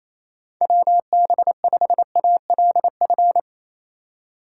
Morse code, heard as a keyed tone, about 27 wpm, 700 Hz. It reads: WB5ALF
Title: USA Call Signs - 6 Characters - 3x Character Spacing - 28wpm